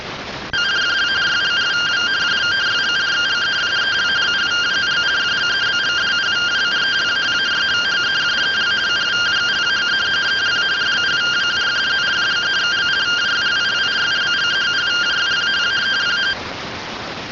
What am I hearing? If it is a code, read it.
Two-tone teleprinter signal, about 170 Hz shift.
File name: findings / CT 205 ax mips, iDose 3 0005.gen-02 rtty rx